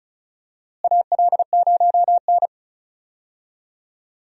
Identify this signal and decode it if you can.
Morse code, heard as a keyed tone, about 34 wpm, 700 Hz. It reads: AL0N
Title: USA Call Signs - 4 Characters - Repeated 3x - 35wpm